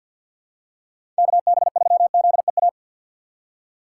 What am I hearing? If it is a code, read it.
Morse code, heard as a keyed tone, about 49 wpm, 690 Hz. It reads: KB3ZEA
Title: USA Call Signs - 2x Character Spacing - 50wpm